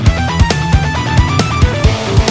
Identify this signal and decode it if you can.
There is no code.